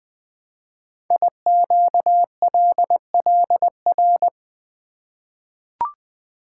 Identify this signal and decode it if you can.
Morse code, heard as a keyed tone, about 20 wpm, 700 Hz. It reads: IQLLR
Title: Code Groups - 5 Letters - 20wpm